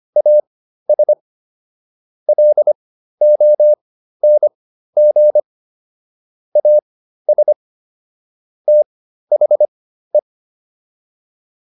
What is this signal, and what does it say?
Morse code, heard as a keyed tone, about 25 wpm, spaced as for 13 wpm, 600 Hz. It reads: AS LONG AS THE